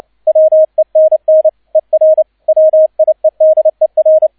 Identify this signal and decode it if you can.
Morse code, heard as a keyed tone, about 29 wpm, 630 Hz, with radio noise. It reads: WENN ER WIEDER
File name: 002 Zweiter Streich 30wpm